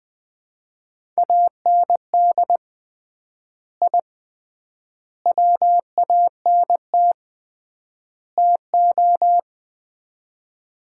Morse code, heard as a keyed tone, about 20 wpm, 700 Hz. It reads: AND I WANT TO
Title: 3x Word-Spacing - Sets of 4 Words - 20wpm